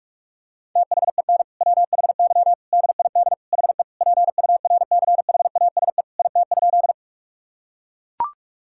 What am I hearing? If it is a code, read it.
Morse code, heard as a keyed tone, about 41 wpm, 700 Hz. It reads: THEN WHY DID HE PURCHASE IT?
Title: Sentences from Top 1000 Words - Review Part 5 - 45wpm